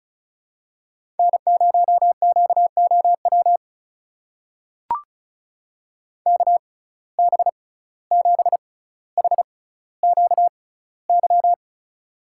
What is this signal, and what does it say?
Morse code, heard as a keyed tone, about 36 wpm, spaced as for 17 wpm, 700 Hz. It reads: N0QOW KB7HQY
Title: USA Call Signs - 6x Character Spacing - 35wpm